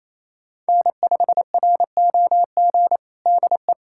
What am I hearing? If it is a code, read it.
Morse code, heard as a keyed tone, about 28 wpm, 700 Hz. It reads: N5ROG DE
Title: USA Callsign DE Callsign - 5 Characters - 28wpm